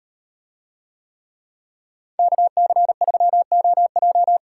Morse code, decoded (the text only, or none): KC3OJ